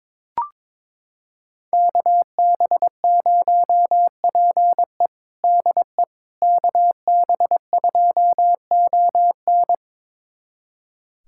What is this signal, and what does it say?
Morse code, heard as a keyed tone, about 21 wpm, 700 Hz. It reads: KB0PE DE KB2ON
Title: USA Callsign DE Callsign - 5 Characters - 22wpm